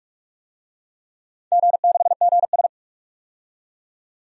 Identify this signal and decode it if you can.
Morse code, heard as a keyed tone, about 45 wpm, 690 Hz. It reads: G6GS
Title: DXCC-100 Call Signs - 4 Characters - 45wpm